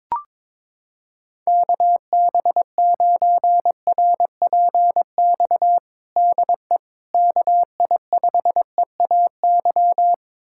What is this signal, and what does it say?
Morse code, heard as a keyed tone, about 23 wpm, 700 Hz. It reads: KB9RPX DE KI5EAY
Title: USA Callsign DE Callsign - 6 Characters - 22wpm